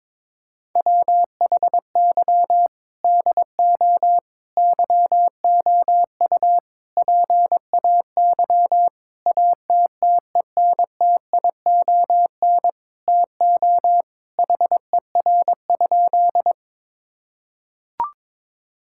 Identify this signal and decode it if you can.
Morse code, heard as a keyed tone, about 22 wpm, 700 Hz. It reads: WHY DO YOU PAY ATTENTION TO HER?
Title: Sentences from Top 1000 Words - Review Part 6 - 22wpm